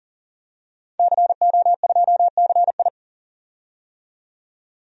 Morse code, heard as a keyed tone, about 37 wpm, 700 Hz. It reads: CO2CI
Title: DXCC-100 Call Signs - 5 Characters - 40wpm